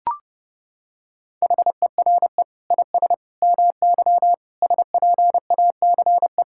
Morse code, heard as a keyed tone, about 31 wpm, 700 Hz. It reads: HERE IS MY SPACE
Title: Sentences from Top 1000 Words - Review Part 5 - 30wpm